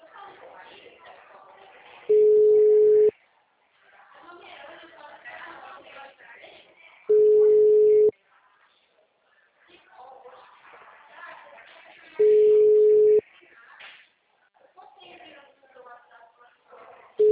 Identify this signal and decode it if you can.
Ringback tone (Europe, 425 Hz), with speech.
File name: out- -2067- - - .63065